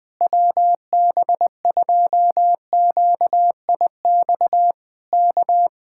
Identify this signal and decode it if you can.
Morse code, heard as a keyed tone, about 20 wpm, 700 Hz. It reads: WB2QIX K
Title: Speed-Racing Common QSO Phrases - 20wpm